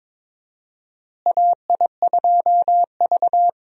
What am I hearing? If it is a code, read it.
Morse code, heard as a keyed tone, about 22 wpm, 700 Hz. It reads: AI2V
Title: USA Call Signs - 4 Characters - Repeated 3x - 22wpm